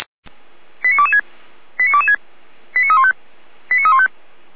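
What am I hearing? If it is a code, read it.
Five-tone selective call (ZVEI), heard as quick runs of five tones, tones of about 70 ms. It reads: 89226 89226 89315 89315
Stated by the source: ZVEI five-tone selective call 89356 received on SDR.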